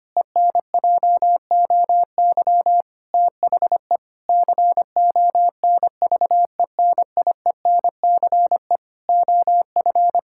Morse code, heard as a keyed tone, about 25 wpm, 700 Hz. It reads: ENJOY THE CONVENIENCE OF
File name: practice / 25wpm / Difficult Sentences with Hints - 25wpm